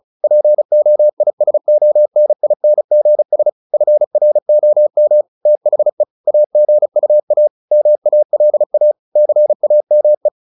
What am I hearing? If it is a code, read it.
Morse code, heard as a keyed tone, about 34 wpm, 600 Hz. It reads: POISONINGS FROM THE AGUA MALA CAME
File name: the old man and the sea / the old man and the sea 1 35wpm